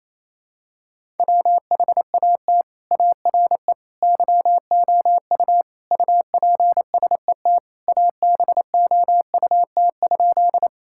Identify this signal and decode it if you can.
Morse code, heard as a keyed tone, about 28 wpm, 700 Hz. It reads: WHAT ARE YOU UPSET ABOUT?
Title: Sentences from Top 1000 Words - Review Part 5 - 28wpm